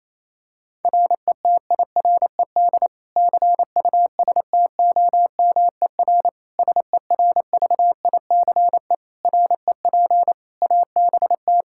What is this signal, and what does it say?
Morse code, heard as a keyed tone, about 28 wpm, 700 Hz. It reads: RETIRED CUSTOMER SERVICE REP ABT
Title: K1USN SST QSO Practice 28wpm